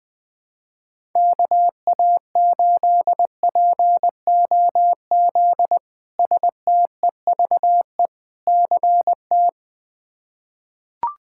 Morse code, heard as a keyed tone, about 21 wpm, 700 Hz. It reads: KA8POZ STEVE CT
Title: K1USN SST QSO Practice - 20wpm